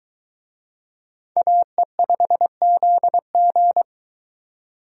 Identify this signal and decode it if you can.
Morse code, heard as a keyed tone, about 20 wpm, 700 Hz. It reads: AE5ZG